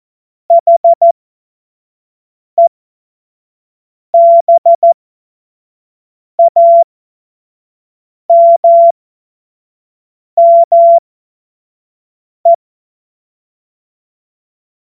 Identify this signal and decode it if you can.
Morse code, heard as a keyed tone, about 14 wpm, spaced as for 5 wpm, 680 Hz. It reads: HEBAMME
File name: Morsekurs / Witze / Witz 06. Januar 2026 25 F70